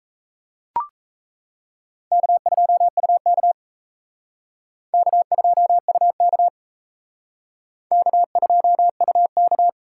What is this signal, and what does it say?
Morse code, heard as a keyed tone, about 34 wpm, 700 Hz. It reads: K2UK K2UK K2UK
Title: Speed Racing - USA 1x2 Call Signs 28wpm